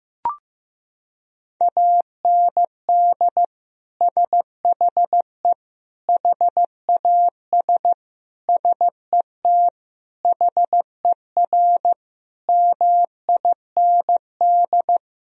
Morse code, heard as a keyed tone, about 15 wpm, 700 Hz. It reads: AND SHE HAS SET HER MIND
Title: Sentences from Top 1000 Words - Review Part 4 - 15wpm